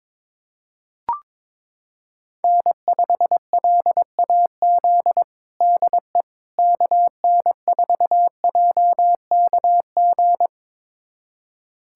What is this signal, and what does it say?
Morse code, heard as a keyed tone, about 22 wpm, 700 Hz. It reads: N5LAZ DE KN4JKG
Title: USA Callsign DE Callsign 22wpm